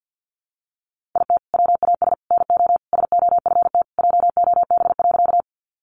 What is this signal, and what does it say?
Morse code, heard as a keyed tone, about 50 wpm, 700 Hz. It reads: IT WAS NO SOFT JOB.
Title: Sentences from Top 1500 Words 50wpm